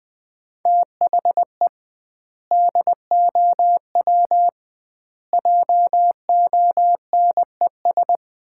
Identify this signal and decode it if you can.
Morse code, heard as a keyed tone, about 20 wpm, 700 Hz. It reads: THE DOW JONES
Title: 2x Word-Spacing - Sets of 5 Words - 20wpm